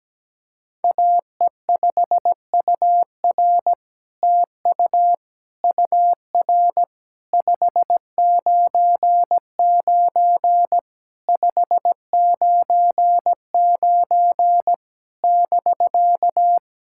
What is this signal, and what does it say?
Morse code, heard as a keyed tone, about 17 wpm, 700 Hz. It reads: AE5UR TU UR 599 599 <BK>
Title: SOTA QSO Practice 17wpm